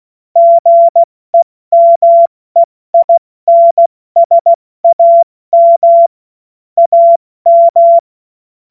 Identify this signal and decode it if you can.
Morse code, heard as a keyed tone, about 16 wpm, spaced as for 14 wpm, 680 Hz. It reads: GEMEINSAM AM
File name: Morsekurs / Witze / Witz 30 Januar 2024 70 F80